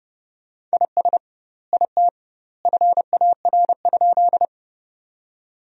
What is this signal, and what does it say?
Morse code, heard as a keyed tone, about 32 wpm, 700 Hz. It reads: IS IT FAR?
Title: 2x Word-Spacing - Sentences from Top 300 Words - 30wpm